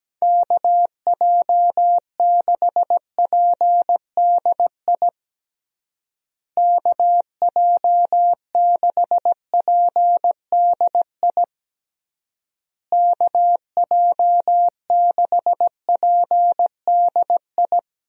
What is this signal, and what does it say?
Morse code, heard as a keyed tone, about 17 wpm, 700 Hz. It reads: KJ6PDI KJ6PDI KJ6PDI
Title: USA Call Signs - 6 Characters - Repeated 3x - 17wpm